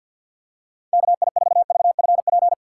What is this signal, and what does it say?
Morse code, heard as a keyed tone, about 53 wpm, 700 Hz. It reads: K I 4 V F P
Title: USA Call Signs - 6 Characters - 6x Character Spacing - 50wpm